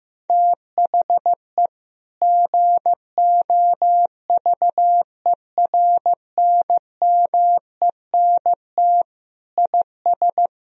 Morse code, heard as a keyed tone, about 15 wpm, 700 Hz. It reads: THE GOVERNMENT IS
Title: Sentences from Top 1000 Words - Review Part 4 - 15wpm